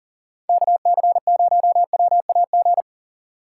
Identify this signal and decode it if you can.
Morse code, heard as a keyed tone, about 40 wpm, 700 Hz. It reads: KC0WAG
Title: USA Call Signs - 6 Characters - 2x Character Spacing - 40wpm